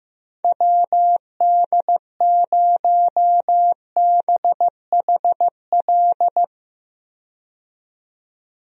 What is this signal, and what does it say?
Morse code, heard as a keyed tone, about 15 wpm, 700 Hz. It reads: WD0BHL
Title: USA Call Signs - 6 Characters - 15wpm